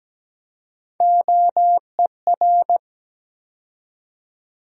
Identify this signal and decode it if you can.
Morse code, heard as a keyed tone, about 17 wpm, 700 Hz. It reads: OER
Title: USA Call Sign Postfixes - 3 Letters 17wpm